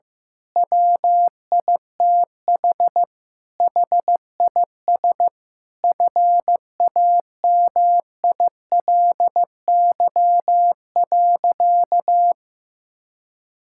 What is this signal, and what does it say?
Morse code, heard as a keyed tone, about 15 wpm, 700 Hz. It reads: WITH HIS FAMILY.
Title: Sentences from Top 1000 Words - Review Part 6 - 15wpm